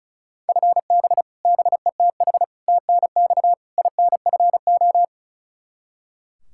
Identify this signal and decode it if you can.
Morse code, heard as a keyed tone, about 36 wpm, 700 Hz. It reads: FB BETH TNX INFO
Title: Practice QSO - Exchange 3 - More Info - 35wpm